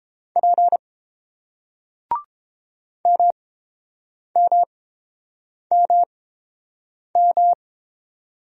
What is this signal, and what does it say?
Morse code, heard as a keyed tone, about 29 wpm, spaced as for 7 wpm, 690 Hz. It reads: P MMMM